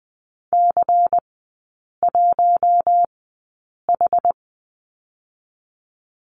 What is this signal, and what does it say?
Morse code, heard as a keyed tone, about 20 wpm, spaced as for 8 wpm, 700 Hz. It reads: C1H
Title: 100 Lesson 17e - Letter u - Three Characters 20wpm